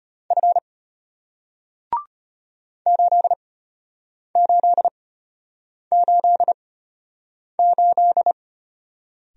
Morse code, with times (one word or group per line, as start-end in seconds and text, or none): 0.3-0.6 F
2.9-8.3 8888